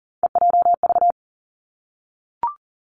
Morse code, heard as a keyed tone, about 40 wpm, 700 Hz. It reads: EJV
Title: Three Challenging Letter-Number Combinations 40wpm